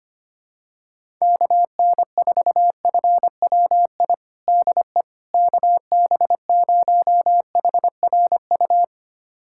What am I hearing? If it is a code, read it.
Morse code, heard as a keyed tone, about 25 wpm, 700 Hz. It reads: KN4FWI DE KB0HRU
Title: USA Callsign DE Callsign - 6 Characters - 25wpm